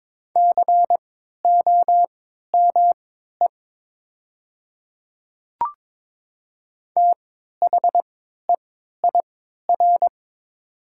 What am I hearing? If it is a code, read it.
Morse code, heard as a keyed tone, about 22 wpm, spaced as for 12 wpm, 700 Hz. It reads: COME THEIR
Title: 3x Character Spacing - Top 100 Words - 22wpm